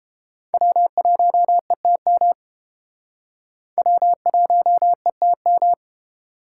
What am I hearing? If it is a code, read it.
Morse code, heard as a keyed tone, about 31 wpm, 700 Hz. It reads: W1ETM W1ETM